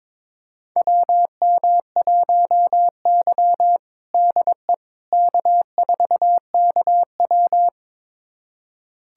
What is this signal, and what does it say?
Morse code, heard as a keyed tone, about 22 wpm, 700 Hz. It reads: WM1Y DE K4KW
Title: USA Callsign DE Callsign - 4 Characters - 22wpm